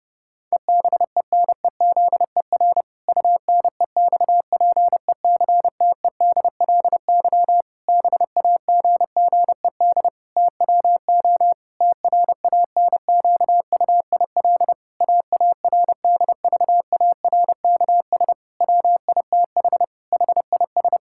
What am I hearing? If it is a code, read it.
Morse code, heard as a keyed tone, about 30 wpm, 700 Hz. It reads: EBENEZER UNEXPECTEDLY BAGGED TWO TRANQUIL AARDVARKS WITH HIS
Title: Pangram Challenge v2 - 30wpm